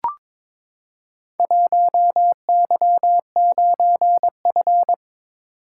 Morse code, heard as a keyed tone, about 22 wpm, 700 Hz. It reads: 1Y9F
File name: practice / 22wpm / Four Letter-Number Combinations - 22wpm